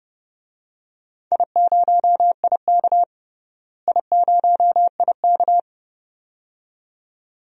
Morse code, heard as a keyed tone, about 30 wpm, 700 Hz. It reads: I0IK I0IK
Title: DXCC-100 Call Signs - 4 Characters - Repeated 2x - 30wpm